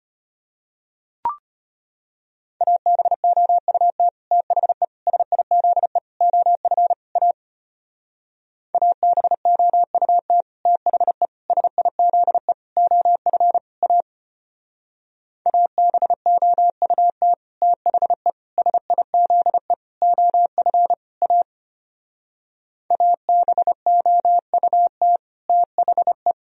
Morse code, heard as a keyed tone, about 33 wpm, 700 Hz. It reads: ABOUT THE SIZE OF A ABOUT THE SIZE OF A ABOUT THE SIZE OF A ABOUT THE